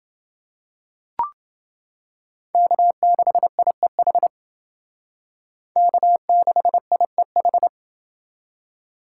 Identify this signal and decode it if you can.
Morse code, heard as a keyed tone, about 28 wpm, 700 Hz. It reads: K6IEH K6IEH